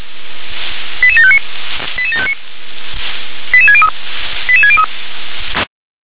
Five-tone selective call (ZVEI), heard as quick runs of five tones, tones of about 70 ms. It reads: 88759 88759 88603 88603